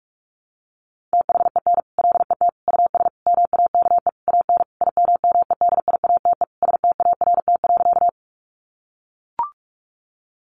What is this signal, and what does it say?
Morse code, heard as a keyed tone, about 44 wpm, 700 Hz. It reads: THEN LET US MAKE AN IMMEDIATE START.
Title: Sentences from Top 1200 Words 45wpm